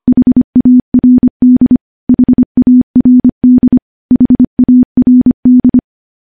Morse code, heard as a keyed tone, about 25 wpm, 250 Hz. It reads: HARD HARD HARD
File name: morse / morse-training-25WPM